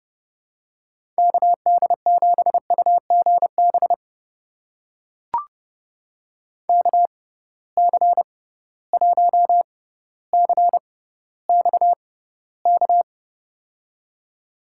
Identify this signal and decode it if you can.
Morse code, heard as a keyed tone, about 30 wpm, spaced as for 15 wpm, 700 Hz. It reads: KD7UGB KC1CXK
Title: USA Call Signs - 6 Characters - 6x Character Spacing - 30wpm